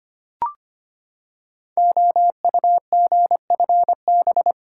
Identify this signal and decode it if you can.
Morse code, heard as a keyed tone, about 25 wpm, 700 Hz. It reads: OUGFB
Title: Code Groups - 5 Letters - 25wpm